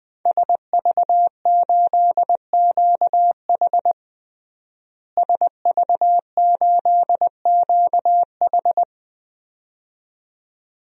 Morse code, heard as a keyed tone, about 20 wpm, 700 Hz. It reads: SV8QH SV8QH
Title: DXCC-100 Call Signs - 5 Characters - Repeated 2x - 20wpm